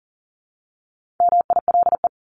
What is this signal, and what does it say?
Morse code, heard as a keyed tone, about 40 wpm, 700 Hz. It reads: MILE
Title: ES:S Suffix - Top 100 Words with Base Words 40wpm